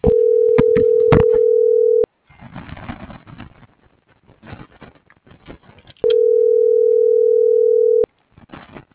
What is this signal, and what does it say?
Ringback tone (North America), 440 + 480 Hz.
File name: CENTRAL TERRESTRE / out ramon.lamorena.2024